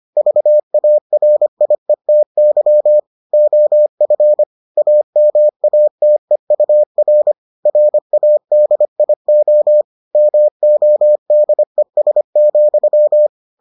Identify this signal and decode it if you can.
Morse code, heard as a keyed tone, about 25 wpm, 600 Hz. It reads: VARIETY OF AMATEUR RADIO MODES,